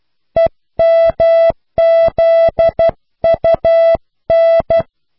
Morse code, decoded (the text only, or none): EMZUN